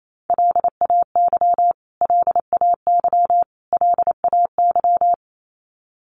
Morse code, heard as a keyed tone, about 28 wpm, 700 Hz. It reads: LAY LAY LAY
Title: Top 500 Words Repeated 28wpm